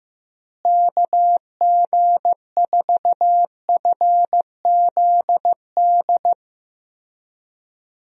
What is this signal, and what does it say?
Morse code, heard as a keyed tone, about 15 wpm, 700 Hz. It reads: KG4FZD
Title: USA Call Signs - 6 Characters - Repeated 3x - 15wpm